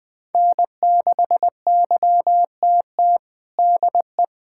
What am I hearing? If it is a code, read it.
Morse code, heard as a keyed tone, about 20 wpm, 700 Hz. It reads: N6YTT DE
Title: USA Callsign DE Callsign 20wpm